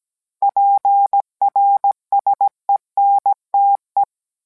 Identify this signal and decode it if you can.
Morse code, heard as a keyed tone, about 17 wpm, 800 Hz. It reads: PRSENTE